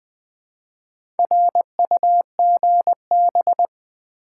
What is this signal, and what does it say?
Morse code, heard as a keyed tone, about 20 wpm, 700 Hz. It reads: RUGB
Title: Build A Word - 1 Letter At A Time - 5-Letter Words - 20wpm